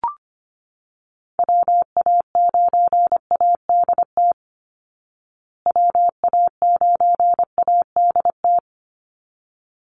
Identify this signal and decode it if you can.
Morse code, heard as a keyed tone, about 25 wpm, 700 Hz. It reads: WA9ADT WA9ADT
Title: Call Signs Repeated 3x - 25wpm